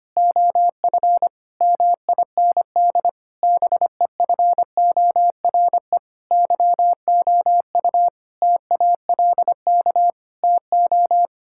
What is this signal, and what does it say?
Morse code, heard as a keyed tone, about 25 wpm, 700 Hz. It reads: OF MIND BEFORE YOU TALK TO
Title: Sentences from Top 1000 Words - Review Part 6 - 25wpm